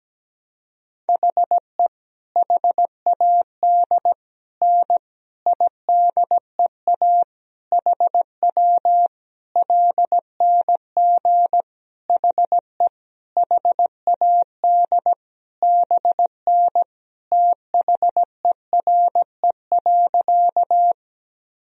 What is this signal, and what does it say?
Morse code, heard as a keyed tone, about 17 wpm, 700 Hz. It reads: HE HAD N IDEA HW LNG HE HAD BN THERE.
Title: Sentences from Top 300 Words and Abbreviations 17wpm